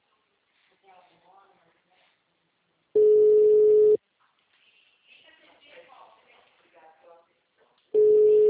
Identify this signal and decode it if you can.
Ringback tone (Europe, 425 Hz), with speech.